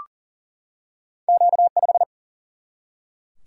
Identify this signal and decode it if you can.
Morse code, heard as a keyed tone, about 40 wpm, 700 Hz. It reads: Q5
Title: Single-Letter Single-Number 40wpm